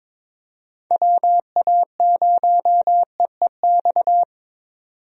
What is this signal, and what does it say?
Morse code, heard as a keyed tone, about 22 wpm, 700 Hz. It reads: WA0EEX
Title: USA Call Signs - 6 Characters - 22wpm